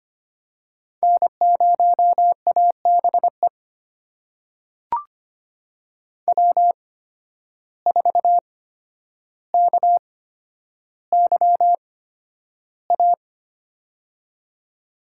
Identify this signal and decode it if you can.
Morse code, heard as a keyed tone, about 25 wpm, spaced as for 11 wpm, 700 Hz. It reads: N0ABE W4KYA
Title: USA Call Signs - 5 Characters - 8x Character Spacing - 25wpm